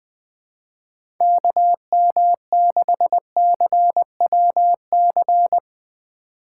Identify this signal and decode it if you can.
Morse code, heard as a keyed tone, about 20 wpm, 700 Hz. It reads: KM6CWC
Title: USA Call Signs - 6 Characters - 4x Character Spacing - 20wpm